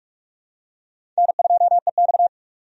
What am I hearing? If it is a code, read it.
Morse code, heard as a keyed tone, about 45 wpm, 700 Hz. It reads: N2EX
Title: USA Call Signs - 4 Characters - 45wpm